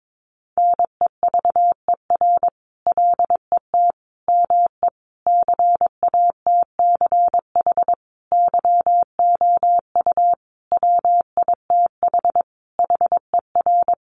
Morse code, heard as a keyed tone, about 22 wpm, 700 Hz. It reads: NEVER LET ME CATCH YOU WITH HER
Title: Sentences from Top 1500 Words 22wpm